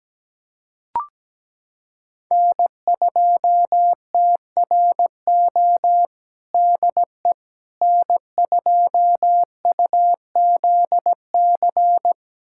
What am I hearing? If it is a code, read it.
Morse code, heard as a keyed tone, about 17 wpm, 700 Hz. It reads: N2TRO DE N2UZC
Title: USA Callsign DE Callsign - 5 Characters - 17wpm